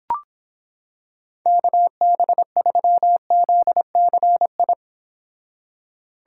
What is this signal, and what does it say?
Morse code, heard as a keyed tone, about 26 wpm, 700 Hz. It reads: KB3ZCI